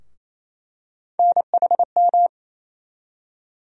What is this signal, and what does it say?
Morse code, heard as a keyed tone, about 28 wpm, 700 Hz. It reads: NHM
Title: USA Call Sign Postfixes - 3 Letters 28wpm